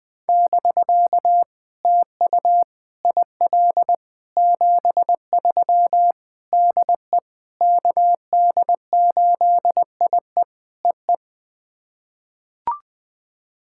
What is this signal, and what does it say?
Morse code, heard as a keyed tone, about 21 wpm, 700 Hz. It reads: <BK> TU IL 73 DE KD8IE EE
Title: POTA QSO Practice - 20wpm